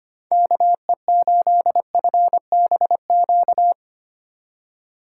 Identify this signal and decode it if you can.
Morse code, heard as a keyed tone, about 25 wpm, 690 Hz. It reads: KE8FBQ
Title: USA Call Signs - 6 Characters - 3x Character Spacing - 25wpm